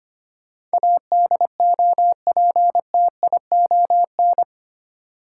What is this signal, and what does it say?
Morse code, heard as a keyed tone, about 25 wpm, 700 Hz. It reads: ADOPTION
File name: Eight Letter Words - 25wpm